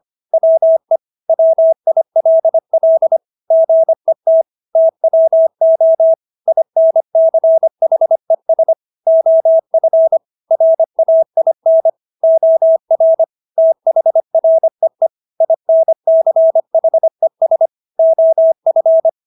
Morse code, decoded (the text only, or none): WE WILL GET TWO INCHES OF RAIN OR THREE INCHES OF